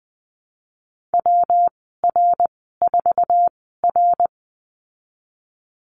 Morse code, heard as a keyed tone, about 20 wpm, spaced as for 14 wpm, 700 Hz. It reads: WR4R
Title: 122 Lesson 20g - Number 3 - Call Signs 20wpm